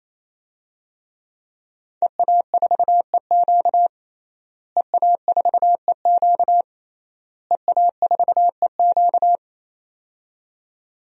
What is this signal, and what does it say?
Morse code, heard as a keyed tone, about 28 wpm, 700 Hz. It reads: EA4EQ EA4EQ EA4EQ
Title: DXCC-100 Call Signs - 5 Characters - Repeated 3x - 28wpm